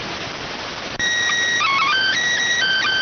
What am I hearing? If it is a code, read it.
SSTV (slow-scan television), PD 290.